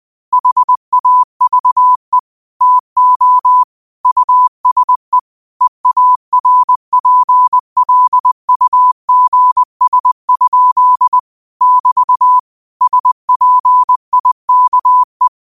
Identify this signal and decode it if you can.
Morse code, heard as a keyed tone, about 20 wpm, 1000 Hz. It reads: HAVE TO USE EARPLUGS? <BT> SPIKE
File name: qotdfiles / 20wpm